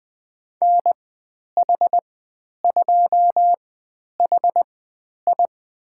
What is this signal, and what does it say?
Morse code, heard as a keyed tone, about 20 wpm, spaced as for 10 wpm, 700 Hz. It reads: NH2HI
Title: 095 Lesson 16g - Letter c - Call Signs 20wpm